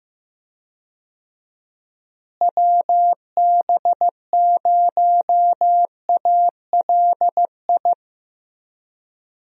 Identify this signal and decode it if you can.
Morse code, heard as a keyed tone, about 15 wpm, 700 Hz. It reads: WB0ALI
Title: USA Call Signs - 6 Characters - Repeated 2x - 15wpm